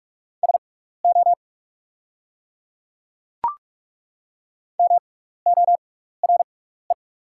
Morse code, heard as a keyed tone, about 43 wpm, spaced as for 15 wpm, 690 Hz. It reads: SO MORE
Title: 6x Character Spacing - Top 100 Words - 45wpm